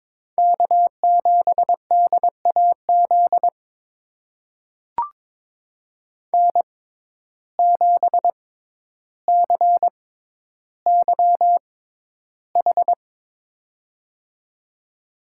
Morse code, heard as a keyed tone, about 22 wpm, spaced as for 11 wpm, 700 Hz. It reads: K7DAZ N7CYH
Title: USA Call Signs - 5 Characters - 6x Character Spacing - 22wpm